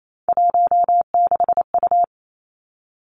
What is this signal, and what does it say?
Morse code, heard as a keyed tone, about 28 wpm, 700 Hz. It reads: 16U
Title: Three Challenging Letter-Number Combinations 28wpm